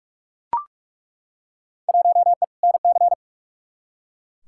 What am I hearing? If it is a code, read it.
Morse code, heard as a keyed tone, about 42 wpm, 700 Hz. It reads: 1E NC
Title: Field Day Exchanges - 45wpm